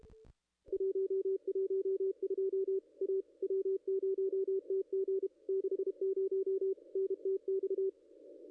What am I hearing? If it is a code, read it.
Morse code, heard as a keyed tone, about 31 wpm, 410 Hz, with radio noise. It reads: 112 A W0TG 60 NTX
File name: contests / ss-cw-k5tr-op-n5rz -0200